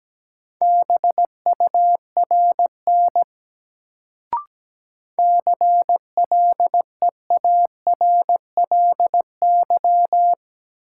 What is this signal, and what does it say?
Morse code, heard as a keyed tone, about 17 wpm, 700 Hz. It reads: BURN CLEARLY
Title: Sentences from Top 1300 Words - New Word Drill Alt - 17wpm